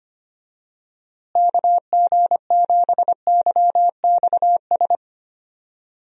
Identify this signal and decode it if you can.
Morse code, heard as a keyed tone, about 25 wpm, 700 Hz. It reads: KG7YXS
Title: USA Call Signs - 6 Characters - Repeated 3x - 25wpm